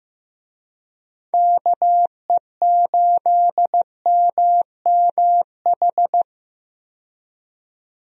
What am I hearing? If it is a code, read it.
Morse code, heard as a keyed tone, about 15 wpm, 700 Hz. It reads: KE8MMH
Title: USA Call Signs - 6 Characters - 15wpm